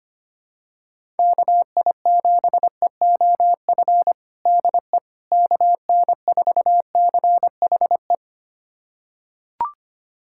Morse code, heard as a keyed tone, about 25 wpm, 700 Hz. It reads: KI7EOF DE KN4CHE
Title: USA Callsign DE Callsign - 6 Characters - 25wpm